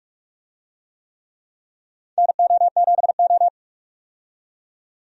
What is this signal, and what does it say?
Morse code, heard as a keyed tone, about 42 wpm, 700 Hz. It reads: NO7O
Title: USA Call Signs - 4 Characters - 45wpm